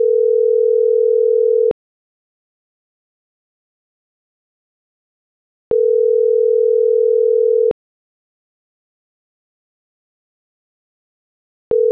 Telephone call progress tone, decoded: ringback tone

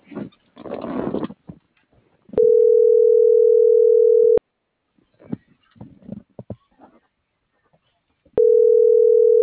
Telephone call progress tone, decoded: ringback tone

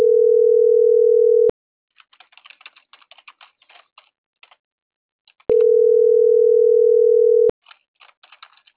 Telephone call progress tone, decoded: ringback tone

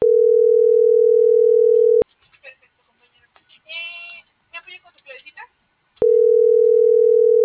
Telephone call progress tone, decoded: ringback tone

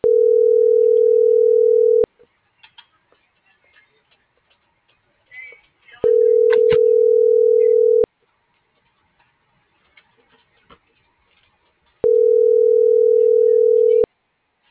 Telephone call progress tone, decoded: ringback tone

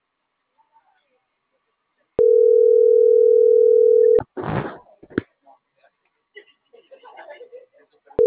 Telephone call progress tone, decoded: ringback tone

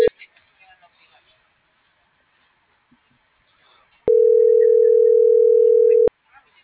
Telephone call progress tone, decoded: ringback tone

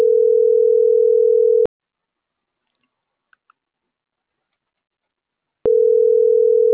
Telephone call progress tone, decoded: ringback tone

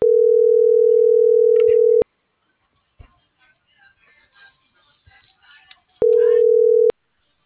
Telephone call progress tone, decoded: ringback tone

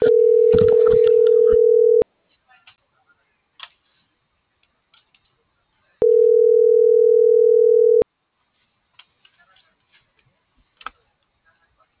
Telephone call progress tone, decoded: ringback tone

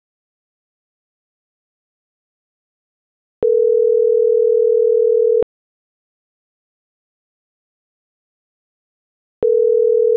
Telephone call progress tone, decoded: ringback tone

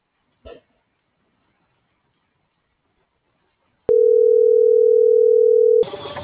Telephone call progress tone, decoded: ringback tone

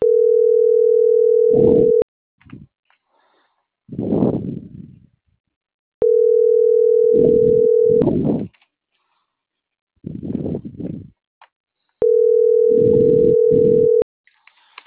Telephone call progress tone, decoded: ringback tone

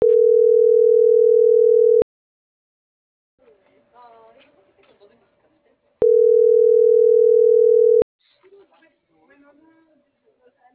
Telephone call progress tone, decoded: ringback tone

